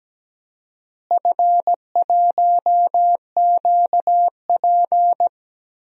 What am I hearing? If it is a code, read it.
Morse code, heard as a keyed tone, about 17 wpm, 700 Hz. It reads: F1QP
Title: DXCC-100 Call Signs - 4 Characters - Repeated 2x - 17wpm